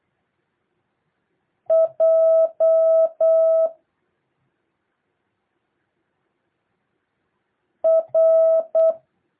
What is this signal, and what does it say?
Morse code, heard as a keyed tone, about 8 wpm, 650 Hz. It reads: J R
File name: jh8cbh / 2023cw